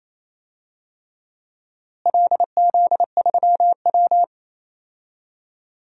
Morse code, heard as a keyed tone, about 28 wpm, 700 Hz. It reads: LZ3W
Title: DXCC-100 Call Signs - 4 Characters - Repeated 3x - 28wpm